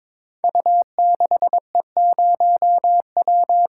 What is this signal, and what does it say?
Morse code, heard as a keyed tone, about 22 wpm, 700 Hz. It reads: U6E0W
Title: Code Groups - 5 Letter:Numbers - 22wpm